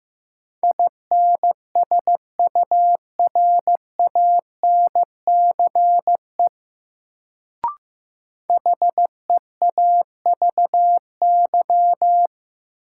Morse code, heard as a keyed tone, about 15 wpm, 700 Hz. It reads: INSURANCE HEAVY
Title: Sentences from Top 1100 Words - New Word Drill Alt - 15wpm